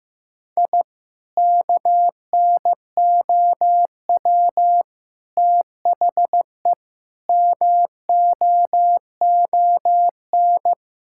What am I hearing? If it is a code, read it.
Morse code, heard as a keyed tone, about 15 wpm, 700 Hz. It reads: I KNOW THE MOON
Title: Sentences from Top 1000 Words - Review Part 6 - 15wpm